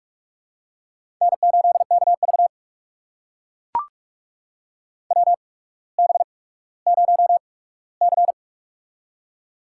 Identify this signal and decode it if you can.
Morse code, heard as a keyed tone, about 44 wpm, spaced as for 19 wpm, 690 Hz. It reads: N8KV WB0C
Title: USA Call Signs - 4 Characters - 8x Character Spacing - 45wpm